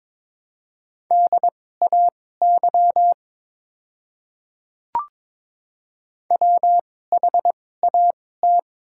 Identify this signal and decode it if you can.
Morse code, heard as a keyed tone, about 22 wpm, spaced as for 16 wpm, 690 Hz. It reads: DAY WHAT
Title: 2x Character Spacing - Top 100 Words - 22wpm